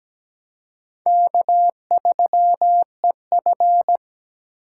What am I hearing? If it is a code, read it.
Morse code, heard as a keyed tone, about 20 wpm, 700 Hz. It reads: K3EF
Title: Speed Racing - USA 1x2 Call Signs 17wpm